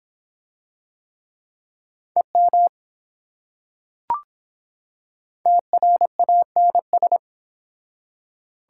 Morse code, heard as a keyed tone, about 24 wpm, 700 Hz. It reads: EM TRANS